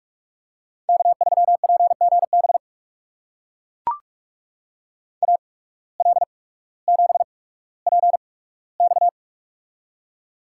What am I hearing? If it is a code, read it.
Morse code, heard as a keyed tone, about 44 wpm, spaced as for 19 wpm, 690 Hz. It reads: K3PGB AL7PX
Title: USA Call Signs - 5 Characters - 8x Character Spacing - 45wpm